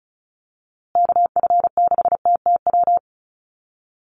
Morse code, heard as a keyed tone, about 34 wpm, 700 Hz. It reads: KF6TTW
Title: USA Call Signs Repeated 3x 35wpm